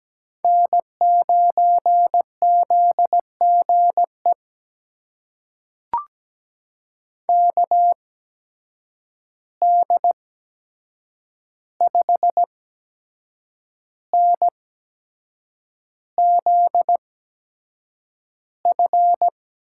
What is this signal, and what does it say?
Morse code, heard as a keyed tone, about 17 wpm, spaced as for 7 wpm, 700 Hz. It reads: N9ZGE KD5NZF
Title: USA Call Signs - 8x Character Spacing - 17wpm